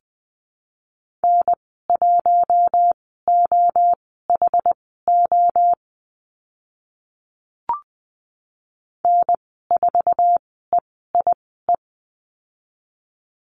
Morse code, heard as a keyed tone, about 20 wpm, spaced as for 15 wpm, 700 Hz. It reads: N1OHO N4EIE
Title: 062 Lesson 11g - Letter h - Call Signs 20wpm